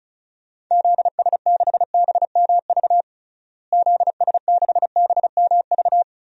Morse code, heard as a keyed tone, about 34 wpm, 700 Hz. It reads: ZS6BMV ZS6BMV
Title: DXCC-100 Call Signs - 6 Characters - Repeated 3x - 35wpm